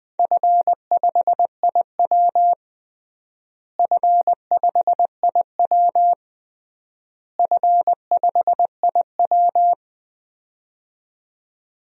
Morse code, heard as a keyed tone, about 20 wpm, 700 Hz. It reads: F5IW F5IW F5IW
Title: DXCC-100 Call Signs - 4 Characters - Repeated 3x - 20wpm